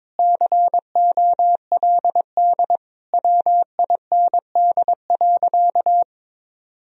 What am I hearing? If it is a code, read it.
Morse code, heard as a keyed tone, about 22 wpm, 700 Hz. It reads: COLD WIND.
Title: Sentences from Top 1000 Words - Review Part 2 - 22wpm